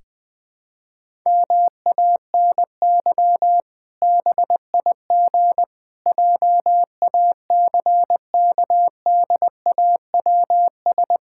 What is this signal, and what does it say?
Morse code, heard as a keyed tone, about 20 wpm, 700 Hz. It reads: MANY BIG JACKDAWS
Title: Pangram Challenge v2 - 20wpm